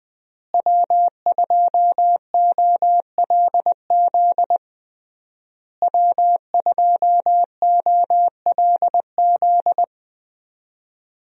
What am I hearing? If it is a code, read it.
Morse code, heard as a keyed tone, about 20 wpm, 700 Hz. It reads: W2OLZ W2OLZ
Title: USA Call Signs - 5 Characters - Repeated 3x - 20wpm